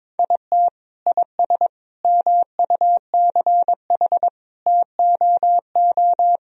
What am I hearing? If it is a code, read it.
Morse code, heard as a keyed tone, about 22 wpm, 700 Hz. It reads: IT IS MUCH TOO
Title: Sentences from Top 1000 Words - Review Part 3 - 22wpm